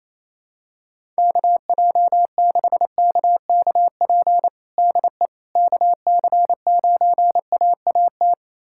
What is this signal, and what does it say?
Morse code, heard as a keyed tone, about 28 wpm, 700 Hz. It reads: KJ6KKP DE KC9AAT
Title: USA Callsign DE Callsign - 6 Characters - 28wpm